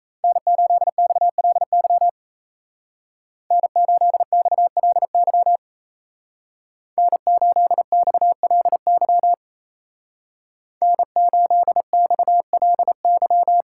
Morse code, heard as a keyed tone, about 34 wpm, 700 Hz. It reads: N8XLY N8XLY N8XLY N8XLY
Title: Speed Racing - USA 1x3 Call Signs 28wpm